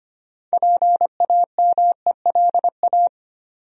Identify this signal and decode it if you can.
Morse code, heard as a keyed tone, about 25 wpm, 700 Hz. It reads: PAMELA
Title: USA Common Names - 25wpm